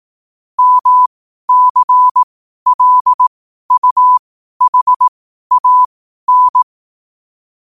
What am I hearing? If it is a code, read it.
Morse code, heard as a keyed tone, about 18 wpm, spaced as for 12 wpm, 1000 Hz. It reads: MCLUHAN
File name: 12wpm